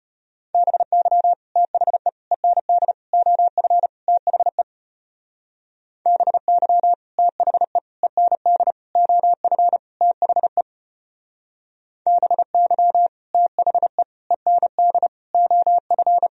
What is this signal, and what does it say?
Morse code, heard as a keyed tone, about 33 wpm, 700 Hz. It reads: BY THE END OF THE BY THE END OF THE BY THE END OF